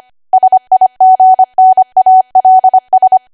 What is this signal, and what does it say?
Morse code, heard as a keyed tone, about 25 wpm, 750 Hz. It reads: SIGNALS
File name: Archive / 25 WPM / 25WPM